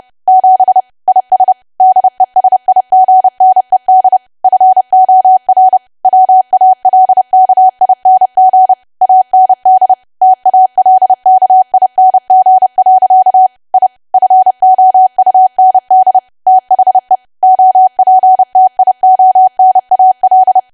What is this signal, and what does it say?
Morse code, heard as a keyed tone, about 30 wpm, 750 Hz. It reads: 7 IS DESIGNED FOR WALKING AND TALKING. I FOUND THE OPTIONAL
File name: cwa / W1AW-Level3 / 30WPM